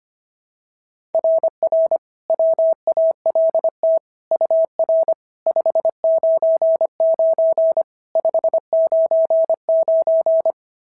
Morse code, heard as a keyed tone, about 25 wpm, 690 Hz. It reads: RR WALT UR 599 599
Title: SOTA QSO Practice 25wpm